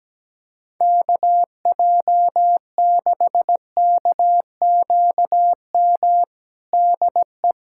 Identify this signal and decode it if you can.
Morse code, heard as a keyed tone, about 17 wpm, 700 Hz. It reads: KJ6KQM DE
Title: K1USN SST QSO Practice 17wpm